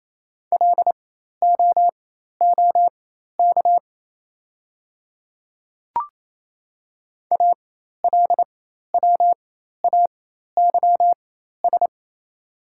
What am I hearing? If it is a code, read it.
Morse code, heard as a keyed tone, about 27 wpm, spaced as for 13 wpm, 700 Hz. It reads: LOOK ALWAYS
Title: 4x Character Spacing - Top 300 Words - 28wpm